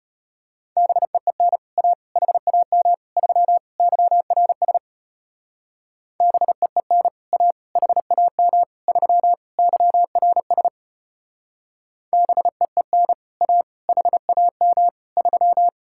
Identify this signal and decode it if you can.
Morse code, heard as a keyed tone, about 33 wpm, 700 Hz. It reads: BEEN A HAM 3 YRS BEEN A HAM 3 YRS BEEN A HAM 3